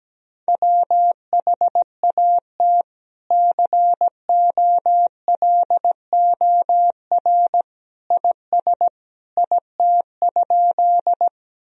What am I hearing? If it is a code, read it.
Morse code, heard as a keyed tone, about 17 wpm, 700 Hz. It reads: WHAT COLOR IS IT?
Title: Sentences from Top 1000 Words - Review Part 2 - 17wpm